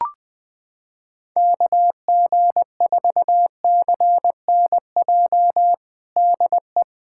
Morse code, heard as a keyed tone, about 20 wpm, 700 Hz. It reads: KG4CNJ DE
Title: USA Callsign DE Callsign - 6 Characters - 20wpm